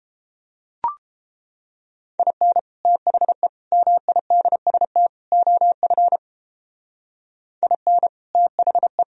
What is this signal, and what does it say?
Morse code, heard as a keyed tone, about 29 wpm, 700 Hz. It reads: IN THE MIDST OF IN THE